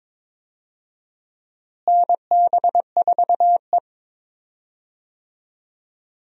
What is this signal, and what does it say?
Morse code, heard as a keyed tone, about 22 wpm, 700 Hz. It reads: NB4E
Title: USA Call Signs - 4 Characters - Repeated 2x - 22wpm